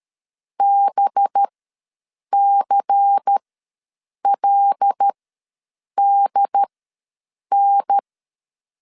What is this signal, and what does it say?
Morse code, heard as a keyed tone, about 13 wpm, spaced as for 7 wpm, 800 Hz. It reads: BCLDN
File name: LEZIONI / 5 LBVC / 5 LBVC ws 12wpm 3159